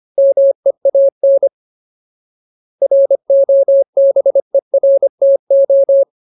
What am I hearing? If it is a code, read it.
Morse code, heard as a keyed tone, about 25 wpm, 550 Hz. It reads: MEAN ROBERTO
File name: Ultimate Word List / 25WPM Ultimate